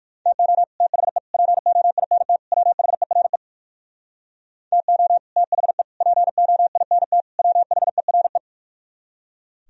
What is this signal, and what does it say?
Morse code, heard as a keyed tone, about 49 wpm, 700 Hz. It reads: TO THE POINT WHERE TO THE POINT WHERE
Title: Speed-Racing - Sets of 4 Words 45wpm